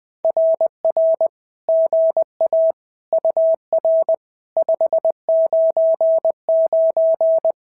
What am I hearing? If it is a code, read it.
Morse code, heard as a keyed tone, about 20 wpm, 650 Hz. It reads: RR GA UR 599
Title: SOTA QSO Practice 20wpm